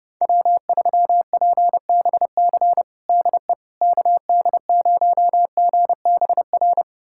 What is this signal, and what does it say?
Morse code, heard as a keyed tone, about 30 wpm, 700 Hz. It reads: W3PBC DE KD0GBR
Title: USA Callsign DE Callsign 30wpm